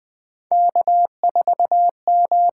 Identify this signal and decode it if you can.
Morse code, heard as a keyed tone, about 20 wpm, 700 Hz. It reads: K4M
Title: USA Call Signs - 5 Characters - 8x Character Spacing - 20wpm